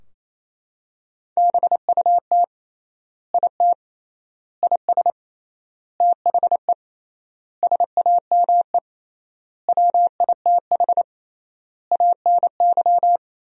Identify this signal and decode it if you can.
Morse code, heard as a keyed tone, about 28 wpm, 700 Hz. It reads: BUT IT IS THE SAME WITH ANY
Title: 3x Word-Spacing - Sentences from Top 300 Words - 28wpm